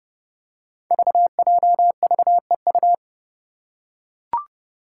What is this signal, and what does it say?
Morse code, heard as a keyed tone, about 30 wpm, 700 Hz. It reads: VJVEU
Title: Five Challenging Letter-Number Combinations 30wpm